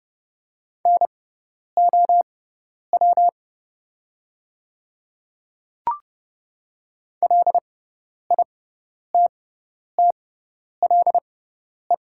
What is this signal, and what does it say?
Morse code, heard as a keyed tone, about 30 wpm, spaced as for 10 wpm, 700 Hz. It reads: NOW LITTLE
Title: 6x Character Spacing - Top 200 Words - 30wpm